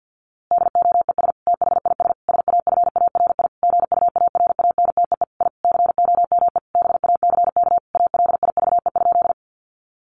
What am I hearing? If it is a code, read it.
Morse code, heard as a keyed tone, about 50 wpm, 700 Hz. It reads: DOES THIS SAFARI GUARANTEE I COME BACK ALIVE?
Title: A Sound Of Thunder 50wpm